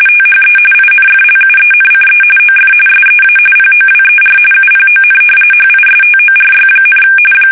RTTY, about 45.45 baud. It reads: THE INFLUENCE OF SEA POWER UPON HISTORY,
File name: fsk850 / usnavy3